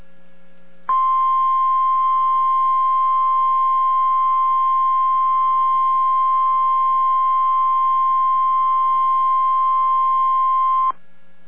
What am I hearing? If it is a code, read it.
Emergency Alert System (SAME) tones (the 1050 Hz attention tone). It is heard from 0.8 to 11.0 s.